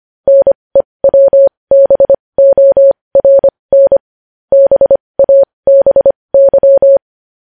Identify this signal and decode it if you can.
Morse code, heard as a keyed tone, about 25 wpm, spaced as for 20 wpm, 550 Hz. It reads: NEWBORN BABY
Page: news-headlines-in-morse-code-at-20-wpm